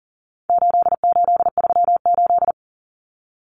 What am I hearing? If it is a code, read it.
Morse code, heard as a keyed tone, about 39 wpm, 700 Hz. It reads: 8838
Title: Numbers - 4-Digit 40wpm